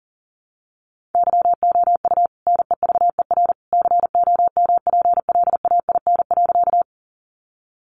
Morse code, heard as a keyed tone, about 40 wpm, 700 Hz. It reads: YOU NEVER COMPLAIN.
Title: Sentences from Top 1300 Words 40wpm